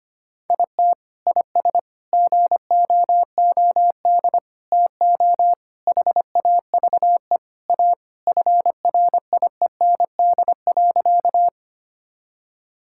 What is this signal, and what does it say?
Morse code, heard as a keyed tone, about 25 wpm, 700 Hz. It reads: IT IS GOOD TO HAVE A FRIEND.
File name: practice / 25wpm / Sentences with 3-Word Phrases - 25wpm